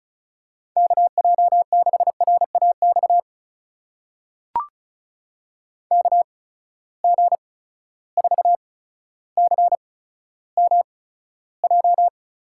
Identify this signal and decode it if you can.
Morse code, heard as a keyed tone, about 36 wpm, spaced as for 15 wpm, 700 Hz. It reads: KJ6RAX KG4CMJ
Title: USA Call Signs - 6 Characters - 8x Character Spacing - 35wpm